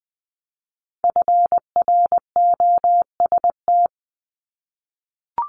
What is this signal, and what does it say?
Morse code, heard as a keyed tone, about 20 wpm, 690 Hz. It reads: FROST
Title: Multi-letter Phonemes 20wpm